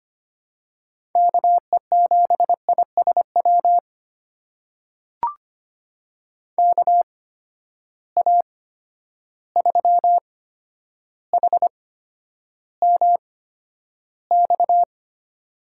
Morse code, heard as a keyed tone, about 25 wpm, spaced as for 11 wpm, 700 Hz. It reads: KE7ISW KA3HMX
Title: USA Call Signs - 6 Characters - 8x Character Spacing - 25wpm